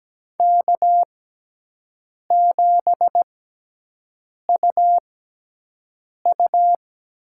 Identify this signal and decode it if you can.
Morse code, heard as a keyed tone, about 17 wpm, spaced as for 6 wpm, 700 Hz. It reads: K7UU
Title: USA Call Signs - 6x Character Spacing - 17wpm